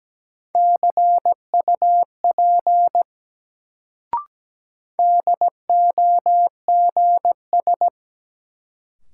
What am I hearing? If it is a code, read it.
Morse code, heard as a keyed tone, about 17 wpm, 700 Hz. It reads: CUP DOGS
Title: Sentences from Top 1600 Words - New Word Drill - 17wpm